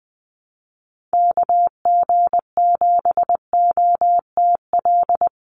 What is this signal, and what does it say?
Morse code, heard as a keyed tone, about 20 wpm, 700 Hz. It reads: KG7OTL
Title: USA Call Signs 20wpm